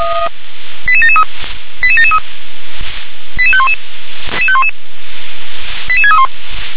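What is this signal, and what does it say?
Five-tone selective call (ZVEI), heard as quick runs of five tones, tones of about 70 ms. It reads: 88773 88773 88511 88631
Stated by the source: Signal received via SDR, ZVEI five-tone.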